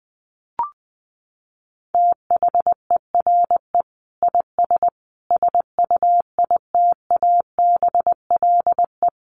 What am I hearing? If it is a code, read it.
Morse code, heard as a keyed tone, about 20 wpm, 700 Hz. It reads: THERE IS SUITABLE
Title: Sentences from Top 1500 Words 20wpm